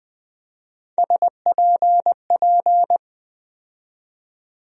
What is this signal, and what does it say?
Morse code, heard as a keyed tone, about 20 wpm, 700 Hz. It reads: SPP
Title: USA Call Sign Postfixes - 3 Letters 20wpm